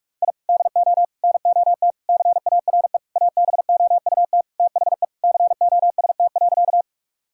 Morse code, heard as a keyed tone, about 48 wpm, 700 Hz. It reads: I DO NOT CARE ABOUT THE COST.
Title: Sentences from Top 1000 Words - Review Part 5 - 45wpm